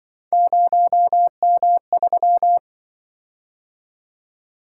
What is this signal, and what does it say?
Morse code, heard as a keyed tone, about 27 wpm, 700 Hz. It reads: 0M3